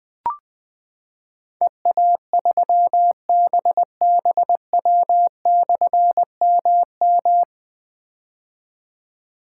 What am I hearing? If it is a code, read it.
Morse code, heard as a keyed tone, about 20 wpm, 700 Hz. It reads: EA3BBW/MM
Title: Difficult Call Signs 20wpm